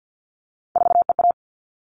Morse code, heard as a keyed tone, about 50 wpm, 700 Hz. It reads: 4EA
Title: All Cut Numbers - 3-Digit 50wpm